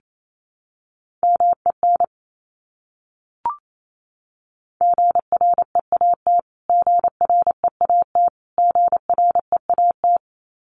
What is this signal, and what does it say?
Morse code, heard as a keyed tone, about 28 wpm, 700 Hz. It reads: MEN GREAT GREAT GREAT
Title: Top 300 Words Repeated 28wpm